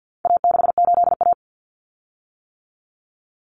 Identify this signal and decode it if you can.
Morse code, heard as a keyed tone, about 50 wpm, 700 Hz. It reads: A68A
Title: Common Cut Numbers - 4-Digit 50wpm